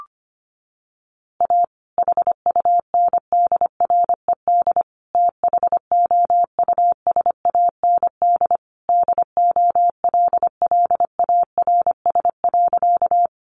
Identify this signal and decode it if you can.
Morse code, heard as a keyed tone, about 25 wpm, 700 Hz. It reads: A HUNDRED THOUSAND DOLLARS.
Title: A Sound Of Thunder 25wpm